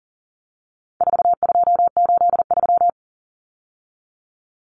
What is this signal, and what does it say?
Morse code, heard as a keyed tone, about 40 wpm, 700 Hz. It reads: 4283
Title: Common Cut Numbers - 4-Digit 40wpm